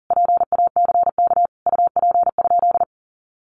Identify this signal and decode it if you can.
Morse code, heard as a keyed tone, about 40 wpm, 700 Hz. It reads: PACK UP?
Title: Sentences from Top 1300 Words 40wpm